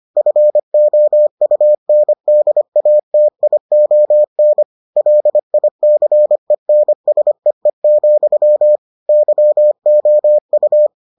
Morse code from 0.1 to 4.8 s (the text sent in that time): FOUNDATION